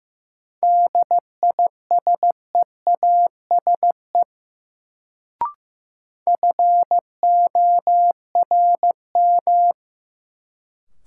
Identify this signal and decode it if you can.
Morse code, heard as a keyed tone, about 15 wpm, 700 Hz. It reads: DISEASE FORM
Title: Sentences from Top 1100 Words - New Word Drill Alt - 15wpm